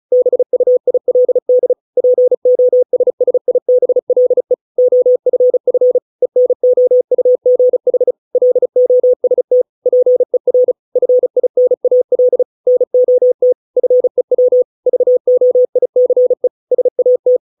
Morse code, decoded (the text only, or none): BUILD POSSIBLE OFF ENOUGH LOST PER FINAL NOT FEW VOICE SAT